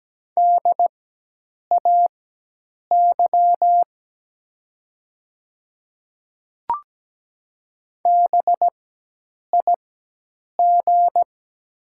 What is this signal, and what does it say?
Morse code, heard as a keyed tone, about 17 wpm, spaced as for 8 wpm, 700 Hz. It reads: DAY BIG
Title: 4x Character Spacing - Top 200 Words - 17wpm